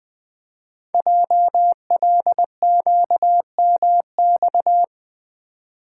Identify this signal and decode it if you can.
Morse code, heard as a keyed tone, about 20 wpm, 700 Hz. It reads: JLQMX
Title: Code Groups - 5 Letters - 20wpm